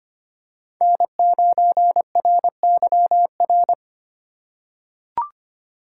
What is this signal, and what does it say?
Morse code, heard as a keyed tone, about 25 wpm, 700 Hz. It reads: N9RYR
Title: USA Call Signs - 3x Character Spacing - 25wpm